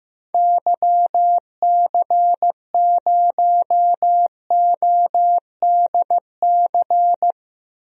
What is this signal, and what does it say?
Morse code, heard as a keyed tone, about 15 wpm, 700 Hz. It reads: YC0ODC
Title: DXCC-100 Call Signs - 6 Characters - 15wpm